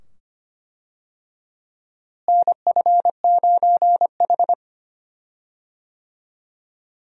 Morse code, heard as a keyed tone, about 25 wpm, 700 Hz. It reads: NF9H
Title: USA Call Signs - 4 Characters - Repeated 2x - 25wpm